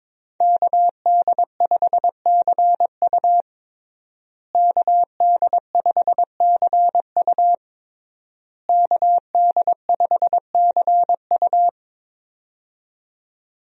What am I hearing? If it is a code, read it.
Morse code, heard as a keyed tone, about 22 wpm, 700 Hz. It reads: KD5CU KD5CU KD5CU
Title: USA Call Signs - 5 Characters - Repeated 3x - 22wpm